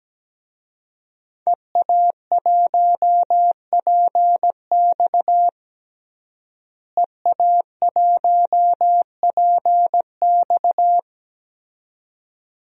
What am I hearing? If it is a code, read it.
Morse code, heard as a keyed tone, about 17 wpm, 700 Hz. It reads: EA1PX EA1PX
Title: DXCC-100 Call Signs - 5 Characters - Repeated 2x - 17wpm